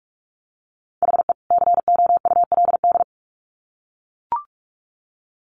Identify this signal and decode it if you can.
Morse code, heard as a keyed tone, about 44 wpm, 700 Hz. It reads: HE COULD
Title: Sets of 2 Words 45wpm